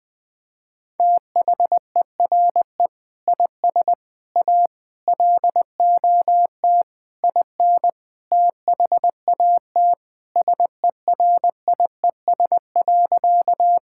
Morse code, heard as a keyed tone, about 21 wpm, 700 Hz. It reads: THERE IS A LOT IN THAT SERIES.
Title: Sentences from Top 1000 Words - Review Part 5 - 20wpm